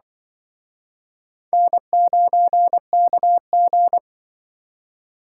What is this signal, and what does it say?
Morse code, heard as a keyed tone, about 25 wpm, 700 Hz. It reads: N9KG